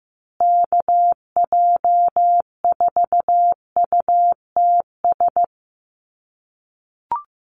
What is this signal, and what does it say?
Morse code, heard as a keyed tone, about 15 wpm, 700 Hz. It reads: KJ4UTS
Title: USA Call Signs Repeated 3x 15wpm